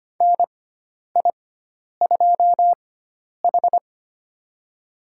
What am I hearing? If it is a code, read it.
Morse code, heard as a keyed tone, about 25 wpm, spaced as for 10 wpm, 700 Hz. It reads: NI2H
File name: Lesson 16g - Letter c - Call Signs 25wpm